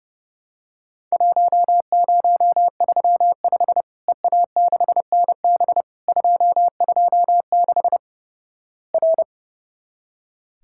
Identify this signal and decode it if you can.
Morse code, heard as a keyed tone, about 32 wpm, 700 Hz. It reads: 1035 EA6NB 226 R
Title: Worked All Europe DX Contest - QTC Exchange - 30wpm